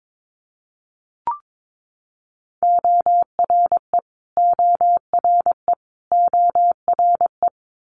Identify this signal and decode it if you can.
Morse code, heard as a keyed tone, about 22 wpm, 700 Hz. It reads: ORE ORE ORE
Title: Multi-letter Phonemes Repeated 3x 22wpm